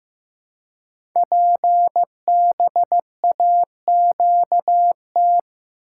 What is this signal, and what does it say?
Morse code, heard as a keyed tone, about 15 wpm, 700 Hz. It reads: PBAQT
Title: Code Groups - 5 Letters - 15wpm